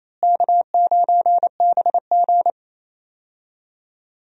Morse code, heard as a keyed tone, about 28 wpm, 700 Hz. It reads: K9BG
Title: USA Call Signs - 4 Characters - Repeated 2x - 28wpm